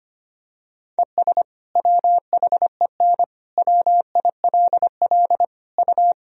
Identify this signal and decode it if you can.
Morse code, heard as a keyed tone, about 25 wpm, 700 Hz. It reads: ES WHEN WILL U
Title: Sentences from Top 100 Words and Abbreviations 25wpm